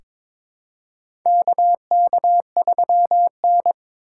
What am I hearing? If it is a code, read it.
Morse code, heard as a keyed tone, about 22 wpm, 700 Hz. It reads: KK3N
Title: USA Call Signs - 4 Characters - 4x Character Spacing - 22wpm